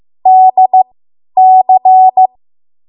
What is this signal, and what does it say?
Morse code, heard as a keyed tone, about 15 wpm, spaced as for 10 wpm, 750 Hz. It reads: DC
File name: Archive / 10 WPM / 10WPM